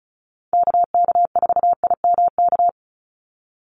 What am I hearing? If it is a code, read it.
Morse code, heard as a keyed tone, about 34 wpm, 700 Hz. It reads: KK4IMK
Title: USA Call Signs 35wpm